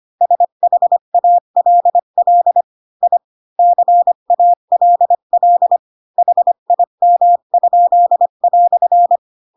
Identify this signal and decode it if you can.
Morse code, heard as a keyed tone, about 25 wpm, 700 Hz. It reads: SHALL I CALL HIM?"
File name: Pinocchio / Pinocchio ch3 20 25 ik1hge